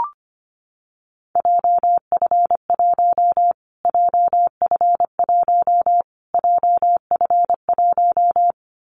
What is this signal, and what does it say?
Morse code, heard as a keyed tone, about 25 wpm, 700 Hz. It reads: JF1 JF1 JF1
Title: Callsign Prefixes for Top 25 DXCC - Repeated 3x - 25wpm